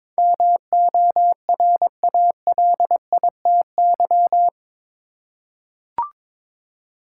Morse code, heard as a keyed tone, about 22 wpm, 700 Hz. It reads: MORALITY
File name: practice / 22wpm / Eight Letter Words - 22wpm